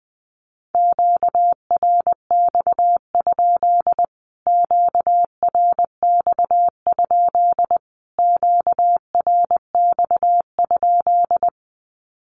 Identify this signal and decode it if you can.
Morse code, heard as a keyed tone, about 20 wpm, 700 Hz. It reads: QRX? QRX? QRX?
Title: Q-Codes Repeated 3x 20wpm